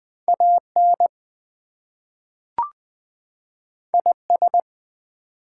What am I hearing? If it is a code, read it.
Morse code, heard as a keyed tone, about 20 wpm, 700 Hz. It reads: AN IS
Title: Two Letter Words - 20wpm